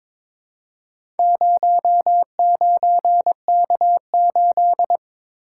Morse code, heard as a keyed tone, about 22 wpm, 700 Hz. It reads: 09K8
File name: practice / 22wpm / Four Letter-Number Combinations - 22wpm